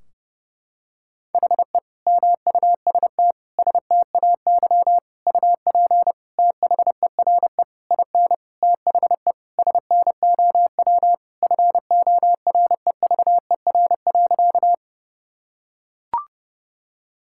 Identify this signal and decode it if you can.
Morse code, heard as a keyed tone, about 30 wpm, 700 Hz. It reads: HE MUST STAY UP THERE IN THE SNOW FOREVER.
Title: Sentences from Top 1600 Words 30wpm